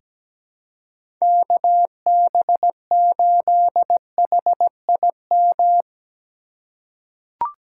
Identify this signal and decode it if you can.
Morse code, heard as a keyed tone, about 17 wpm, 700 Hz. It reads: KB8HIM
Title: USA Call Signs - 6 Characters - 6x Character Spacing - 17wpm